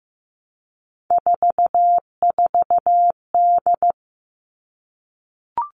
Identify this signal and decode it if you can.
Morse code, heard as a keyed tone, about 15 wpm, 700 Hz. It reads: 44D
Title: All Cut Numbers - 3-Digit 15wpm